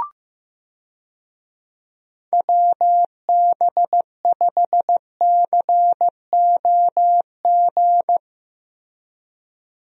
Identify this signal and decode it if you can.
Morse code, heard as a keyed tone, about 15 wpm, 700 Hz. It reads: WB5COG
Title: USA Call Signs - 6 Characters - Repeated 3x - 15wpm